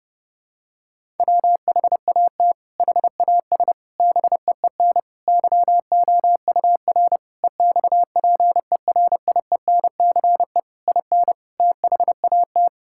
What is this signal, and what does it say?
Morse code, heard as a keyed tone, about 30 wpm, 700 Hz. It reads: WHAT HAS BEEN YOUR EXPERIENCE IN THAT
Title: Sentences from Top 1000 Words - Review Part 6 - 30wpm